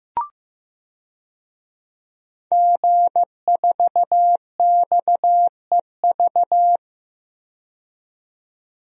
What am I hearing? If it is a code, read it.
Morse code, heard as a keyed tone, about 15 wpm, 700 Hz. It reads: G4XEV
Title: DXCC-100 Call Signs - 5 Characters - 15wpm